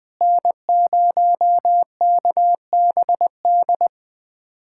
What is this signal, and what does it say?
Morse code, heard as a keyed tone, about 20 wpm, 700 Hz. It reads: N0KBD
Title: K1USN SST QSO Practice - 20wpm